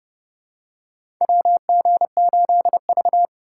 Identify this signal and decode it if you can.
Morse code, heard as a keyed tone, about 30 wpm, 700 Hz. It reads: WG8V
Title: USA Call Signs - 2x Character Spacing - 30wpm